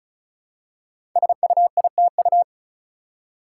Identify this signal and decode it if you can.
Morse code, heard as a keyed tone, about 36 wpm, 700 Hz. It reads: SUITU
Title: Code Groups - 5 Letters - 35wpm